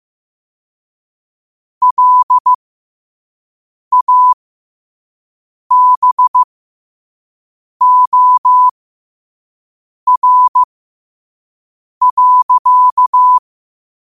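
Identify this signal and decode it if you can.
Morse code, heard as a keyed tone, about 15 wpm, spaced as for 5 wpm, 1000 Hz. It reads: LABOR.
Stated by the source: Morse at 5 wpm.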